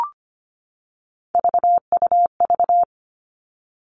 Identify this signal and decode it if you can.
Morse code, heard as a keyed tone, about 25 wpm, 700 Hz. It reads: VUV
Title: Three Challenging Letter-Number Combinations 25wpm